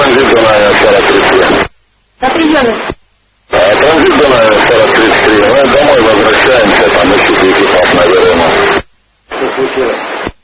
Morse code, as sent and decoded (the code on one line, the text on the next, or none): none